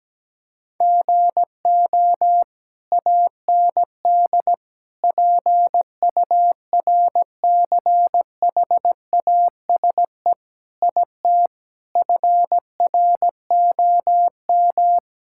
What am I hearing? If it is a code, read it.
Morse code, heard as a keyed tone, about 17 wpm, 700 Hz. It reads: GO AND PURCHASE IT FROM
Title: Sentences from Top 1000 Words - Review Part 5 - 17wpm